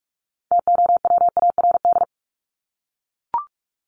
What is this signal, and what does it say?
Morse code, heard as a keyed tone, about 44 wpm, 700 Hz. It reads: TOWARD
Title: ES:S Suffix - Top 100 Words with Base Words 45wpm